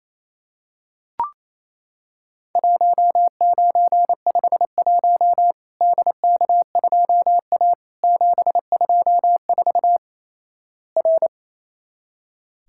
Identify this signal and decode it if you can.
Morse code, heard as a keyed tone, about 28 wpm, 700 Hz. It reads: 1951 DK2A 724 R
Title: Worked All Europe DX Contest - QTC Exchange - 28wpm